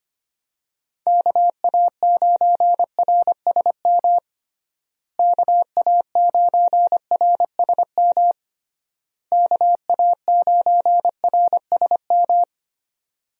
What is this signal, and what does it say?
Morse code, heard as a keyed tone, about 25 wpm, 700 Hz. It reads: KA9RSM KA9RSM KA9RSM
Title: USA Call Signs - 6 Characters - Repeated 3x - 25wpm